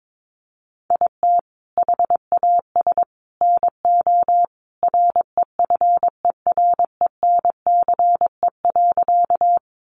Morse code, heard as a keyed tone, about 22 wpm, 700 Hz. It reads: IT HAS NO REFERENCE.
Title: Sentences from Top 1200 Words 22wpm